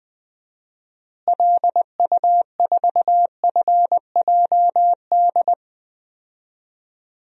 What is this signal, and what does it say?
Morse code, heard as a keyed tone, about 21 wpm, 700 Hz. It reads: LU4FJD
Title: DXCC-100 Call Signs - 6 Characters - Repeated 3x - 20wpm